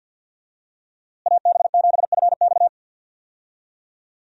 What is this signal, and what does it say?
Morse code, heard as a keyed tone, about 46 wpm, 700 Hz. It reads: AB7FX
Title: USA Call Signs - 5 Characters - Repeated 3x - 50wpm